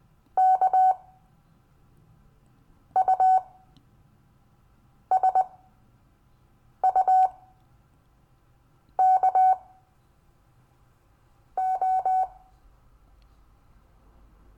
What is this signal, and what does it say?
Morse code, heard as a keyed tone, about 20 wpm, spaced as for 5 wpm, 750 Hz. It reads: KUSUKO
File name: cw / Lesson15 five forwardslash